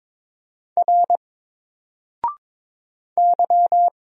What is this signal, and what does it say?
Morse code, heard as a keyed tone, about 22 wpm, 700 Hz. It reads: R Y
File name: Single-Letter-Number-22wpm